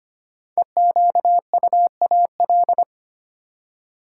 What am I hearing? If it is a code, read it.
Morse code, heard as a keyed tone, about 25 wpm, 700 Hz. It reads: EQUAL
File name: 5 Letter Words - First 3 Letters Given - 25wpm